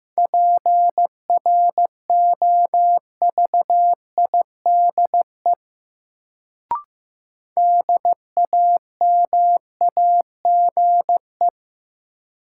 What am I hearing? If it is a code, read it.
Morse code, heard as a keyed tone, about 15 wpm, 700 Hz. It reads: PROVIDE DAMAGE
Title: Sentences from Top 1100 Words - New Word Drill Alt - 15wpm